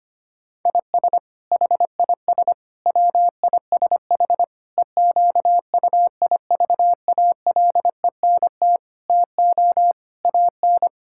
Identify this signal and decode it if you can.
Morse code, heard as a keyed tone, about 25 wpm, 700 Hz. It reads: IS HIS WISH EQUIVALENT TO AN
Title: Sentences from Top 1600 Words 25wpm